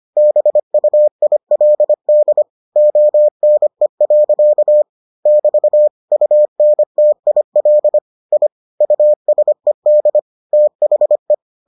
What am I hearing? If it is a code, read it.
Morse code, heard as a keyed tone, about 25 wpm, 600 Hz. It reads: BUILD ONE. <BT> UNTIL I USED THE